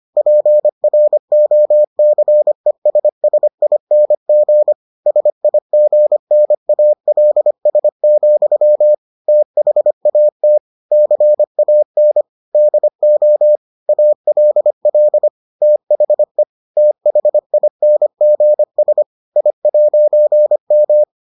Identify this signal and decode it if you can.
Morse code, heard as a keyed tone, about 25 wpm, 600 Hz. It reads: PROCESSING SIGNALS, THAT CAN DO ALL THE THINGS I'M